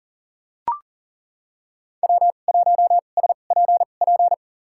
Morse code, heard as a keyed tone, about 40 wpm, spaced as for 29 wpm, 700 Hz. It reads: W1SPP
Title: USA Call Signs - 5 Characters - 2x Character Spacing - 40wpm